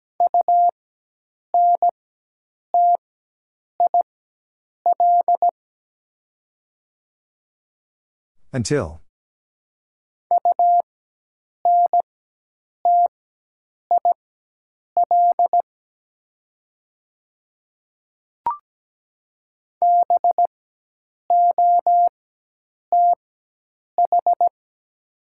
Morse code, heard as a keyed tone, about 17 wpm, spaced as for 8 wpm, 700 Hz. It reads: UNTIL UNTIL BOTH
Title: 4x Character Spacing - Top 300 Words - 17wpm